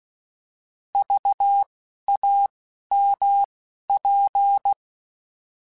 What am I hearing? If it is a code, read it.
Morse code, heard as a keyed tone, about 16 wpm, spaced as for 11 wpm, 800 Hz. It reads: VAMP
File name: Alphabetical / Alphabetical mp3s / Vs @10